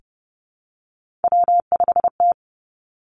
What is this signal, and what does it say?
Morse code, heard as a keyed tone, about 30 wpm, 700 Hz. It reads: W5T
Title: Three Challenging Letter-Number Combinations 30wpm